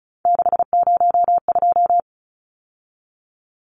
Morse code, heard as a keyed tone, about 34 wpm, 700 Hz. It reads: 602
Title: Numbers - 3-Digit 35wpm